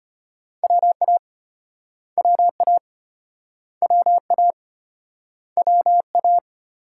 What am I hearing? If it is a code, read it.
Morse code, heard as a keyed tone, about 34 wpm, 700 Hz. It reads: WA WA WA WA